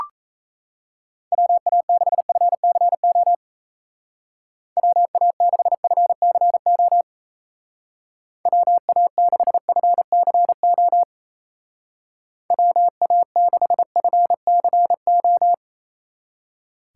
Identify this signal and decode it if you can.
Morse code, heard as a keyed tone, about 33 wpm, 700 Hz. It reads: WA6FCO WA6FCO WA6FCO WA6FCO
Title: Speed Racing - USA 2x3 Call Signs 28wpm